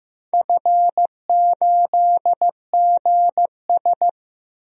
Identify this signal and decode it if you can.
Morse code, heard as a keyed tone, about 16 wpm, 700 Hz. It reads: F8GS
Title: DXCC-100 Call Signs - 4 Characters - 15wpm